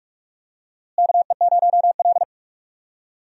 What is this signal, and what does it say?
Morse code, heard as a keyed tone, about 41 wpm, 700 Hz. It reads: KE0L
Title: USA Call Signs - 4 Characters - Repeated 3x - 45wpm